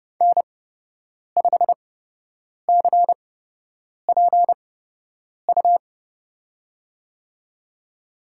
Morse code, heard as a keyed tone, about 30 wpm, spaced as for 8 wpm, 700 Hz. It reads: N5CPU
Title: USA Call Signs - 8x Character Spacing - 30wpm